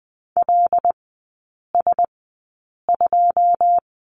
Morse code, heard as a keyed tone, about 20 wpm, spaced as for 8 wpm, 700 Hz. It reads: LS2
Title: 114 Lesson 19e - Letter w - Three Characters 20wpm